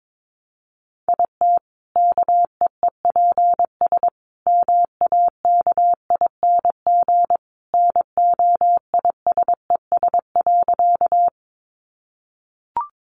Morse code, heard as a keyed tone, about 22 wpm, 700 Hz. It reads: IT KEEPS MAKING NOISES.
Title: Additional Easy Phrases 22wpm